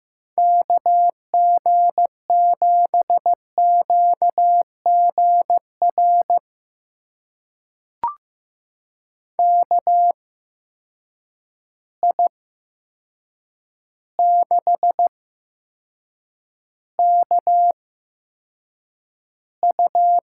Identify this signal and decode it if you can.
Morse code, heard as a keyed tone, about 15 wpm, spaced as for 7 wpm, 700 Hz. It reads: KG7QGR KI6KU
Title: USA Call Signs - 6 Characters - 8x Character Spacing - 15wpm